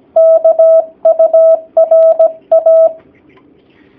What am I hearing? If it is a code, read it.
Morse code, heard as a keyed tone, about 17 wpm, 650 Hz. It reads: KURA
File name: jh8cbh / 2019cw / sp16- w